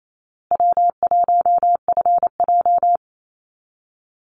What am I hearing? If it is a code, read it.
Morse code, heard as a keyed tone, about 28 wpm, 700 Hz. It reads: W1FJ
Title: Easy Call Signs Repeated 3x - 28wpm